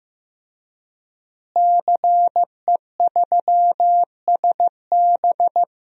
Morse code, heard as a keyed tone, about 16 wpm, 700 Hz. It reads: CE3SB
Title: DXCC-100 Call Signs - 5 Characters - Repeated 2x - 15wpm